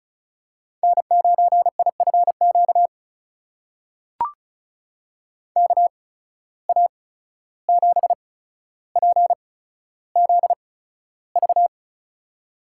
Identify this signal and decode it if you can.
Morse code, heard as a keyed tone, about 36 wpm, spaced as for 14 wpm, 700 Hz. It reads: N9IFQ KA7PZV
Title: USA Call Signs - 8x Character Spacing - 35wpm